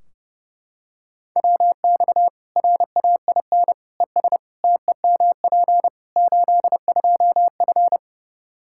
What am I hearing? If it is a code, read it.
Morse code, heard as a keyed tone, about 30 wpm, 700 Hz. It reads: WX RAIN ES TEMP 82F
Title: Practice QSO - Exchange 2 - Info - 30wpm